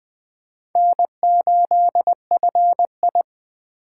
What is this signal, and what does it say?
Morse code, heard as a keyed tone, about 20 wpm, 700 Hz. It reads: N8FI